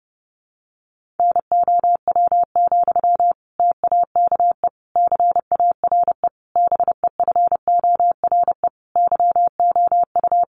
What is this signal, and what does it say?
Morse code, heard as a keyed tone, about 30 wpm, 700 Hz. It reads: NOW, TAKE CARE BEFORE YOU
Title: Sentences from Top 1500 Words 30wpm